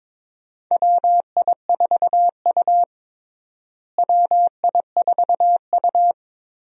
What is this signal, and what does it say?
Morse code, heard as a keyed tone, about 22 wpm, 700 Hz. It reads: WI4U WI4U
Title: USA Call Signs - 4 Characters - Repeated 3x - 22wpm